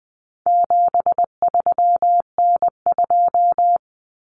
Morse code, heard as a keyed tone, about 20 wpm, 700 Hz. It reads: 73N2
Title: Common Cut Numbers - 4-Digit 20wpm